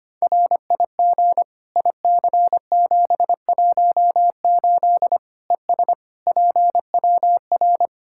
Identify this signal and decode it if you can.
Morse code, heard as a keyed tone, about 25 wpm, 700 Hz. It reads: RIG IC718 ES PWR
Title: Practice QSO - Exchange 2 - Info - 25wpm